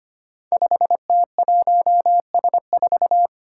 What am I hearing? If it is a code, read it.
Morse code, heard as a keyed tone, about 25 wpm, 700 Hz. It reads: 5T1S4
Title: Five Challenging Letter-Number Combinations 25wpm